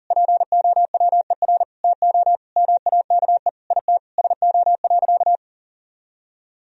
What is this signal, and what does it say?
Morse code, heard as a keyed tone, about 40 wpm, 700 Hz. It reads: POWER TO MAKE IT SO.
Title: Sentences from Top 1000 Words - Review Part 3 - 40wpm